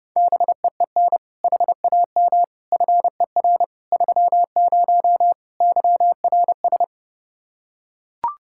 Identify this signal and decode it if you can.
Morse code, heard as a keyed tone, about 30 wpm, 700 Hz. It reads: BEEN HAM FER 30 YRS